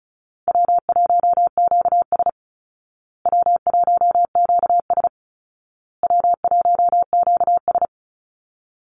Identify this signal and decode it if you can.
Morse code, heard as a keyed tone, about 34 wpm, 700 Hz. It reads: W1QS W1QS W1QS
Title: Easy Call Signs Repeated 3x - 35wpm